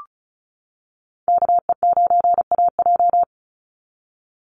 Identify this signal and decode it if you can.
Morse code, heard as a keyed tone, about 34 wpm, 700 Hz. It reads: KE9AJ
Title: NA SOTA Chaser Callsigns 35wpm